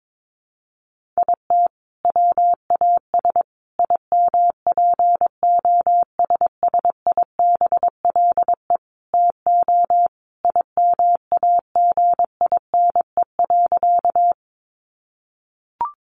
Morse code, heard as a keyed tone, about 22 wpm, 700 Hz. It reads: IT WAS IMPOSSIBLE TO IMAGINE.
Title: Sentences from Top 1400 Words 22wpm